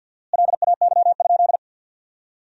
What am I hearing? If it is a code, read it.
Morse code, heard as a keyed tone, about 54 wpm, 700 Hz. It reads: LAY?
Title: 4x Word-Spacing - Sentences from Top 400 Words - 50wpm